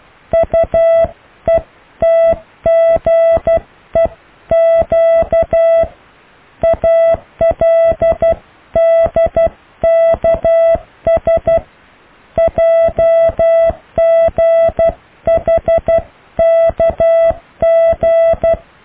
Morse code, heard as a keyed tone, about 12 wpm, 660 Hz. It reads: UETGEQ ALDKS JGHKG
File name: cw / Alle letters